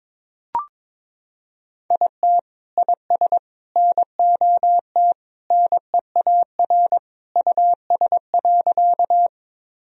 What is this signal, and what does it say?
Morse code, heard as a keyed tone, about 22 wpm, 700 Hz. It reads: IT IS NOT NEAR US.
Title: Sentences from Top 1000 Words - Review Part 1 - 22wpm